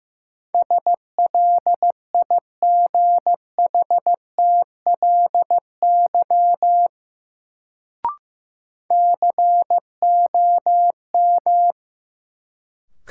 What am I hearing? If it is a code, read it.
Morse code, heard as a keyed tone, about 15 wpm, 700 Hz. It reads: SLIGHTLY COM
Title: Sentences from Top 1300 Words - New Word Drill Alt - 15wpm